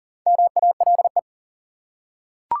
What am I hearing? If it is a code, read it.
Morse code, heard as a keyed tone, about 40 wpm, 700 Hz. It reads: MALE
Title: Four Letter Words - 40wpm